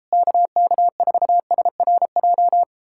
Morse code, heard as a keyed tone, about 37 wpm, 700 Hz. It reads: KK4SRJ